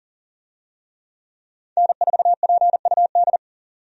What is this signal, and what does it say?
Morse code, heard as a keyed tone, about 40 wpm, 700 Hz. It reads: N4PUD
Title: USA Call Signs - 5 Characters - 40wpm